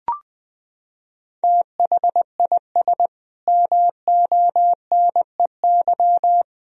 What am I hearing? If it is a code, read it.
Morse code, heard as a keyed tone, about 21 wpm, 700 Hz. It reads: THIS MONEY
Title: Sentences from Top 400 Words and Abbreviations 20wpm